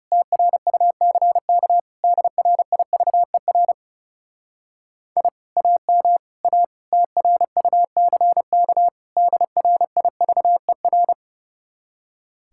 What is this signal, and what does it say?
Morse code, heard as a keyed tone, about 37 wpm, 700 Hz. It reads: TRUCK DRIVER I AM A TRUCK DRIVER
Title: Speed-Racing Common QSO Phrases - 30wpm